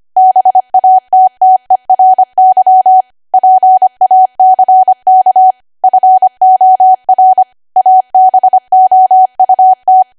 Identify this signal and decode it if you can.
Morse code, heard as a keyed tone, about 25 wpm, 750 Hz. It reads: BATTERY PACK FOR ABOUT
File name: Archive / 25 WPM / 25WPM